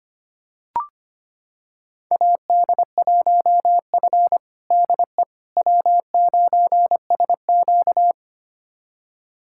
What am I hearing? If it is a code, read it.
Morse code, heard as a keyed tone, about 25 wpm, 700 Hz. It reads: AD1F DE W9SQ
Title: USA Callsign DE Callsign - 4 Characters - 25wpm